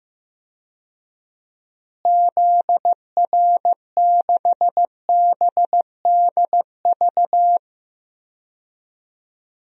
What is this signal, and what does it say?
Morse code, heard as a keyed tone, about 16 wpm, 700 Hz. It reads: ZR6BDV
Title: DXCC-100 Call Signs - 6 Characters - Repeated 2x - 15wpm